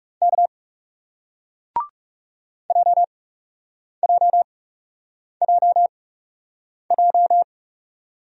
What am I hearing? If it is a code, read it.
Morse code, heard as a keyed tone, about 40 wpm, spaced as for 8 wpm, 700 Hz. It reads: K JJJJ